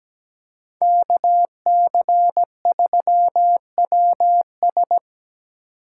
Morse code, heard as a keyed tone, about 20 wpm, 700 Hz. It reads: KC3WS
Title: Speed Racing - USA 2x2 Call Signs 17wpm